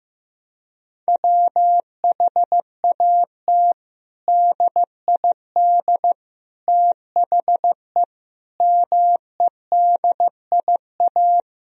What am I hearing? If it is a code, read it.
Morse code, heard as a keyed tone, about 16 wpm, 700 Hz. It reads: WHAT DID THE MEDIA
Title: Sentences from Top 1000 Words - Review Part 4 - 15wpm